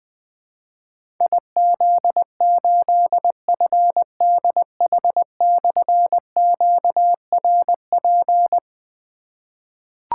Morse code, heard as a keyed tone, about 20 wpm, 700 Hz. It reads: IZ8FDH/QRP
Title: Difficult Call Signs 20wpm